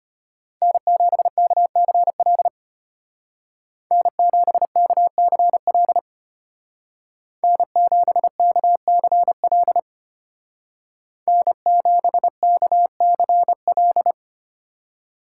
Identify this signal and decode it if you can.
Morse code, heard as a keyed tone, about 33 wpm, 700 Hz. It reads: N7KCL N7KCL N7KCL N7KCL